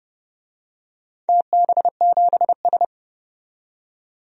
Morse code, heard as a keyed tone, about 30 wpm, 700 Hz. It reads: TB7S
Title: DXCC-100 Call Signs - 4 Characters - 30wpm